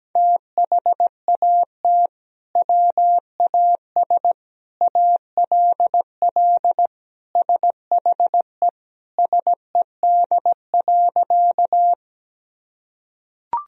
Morse code, heard as a keyed tone, about 17 wpm, 700 Hz. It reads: THAT WAS ALL SHE SED.
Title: Sentences from Top 100 Words and Abbreviations 17wpm